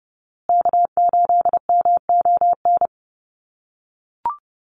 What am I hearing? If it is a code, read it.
Morse code, heard as a keyed tone, about 30 wpm, 700 Hz. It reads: K8MON
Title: USA Call Signs Repeated 2x 30wpm